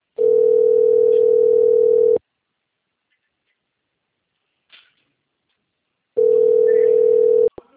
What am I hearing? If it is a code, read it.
Ringback tone (North America), 440 + 480 Hz.